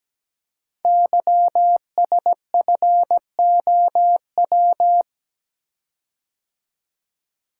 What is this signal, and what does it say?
Morse code, heard as a keyed tone, about 17 wpm, 700 Hz. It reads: YSFOW
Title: Code Groups - 5 Letter:Numbers - 17wpm